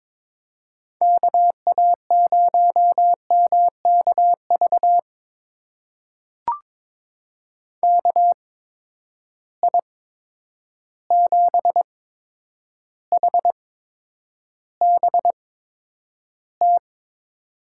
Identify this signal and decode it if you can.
Morse code, heard as a keyed tone, about 22 wpm, spaced as for 9 wpm, 700 Hz. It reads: KA0MKV KI7HBT
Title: USA Call Signs - 6 Characters - 8x Character Spacing - 22wpm